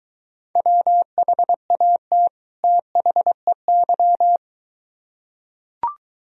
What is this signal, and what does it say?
Morse code, heard as a keyed tone, about 22 wpm, 700 Hz. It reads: WHAT THEY